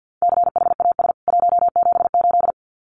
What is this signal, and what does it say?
Morse code, heard as a keyed tone, about 50 wpm, 700 Hz. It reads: CHAS 178
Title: CWT Exchange 50wpm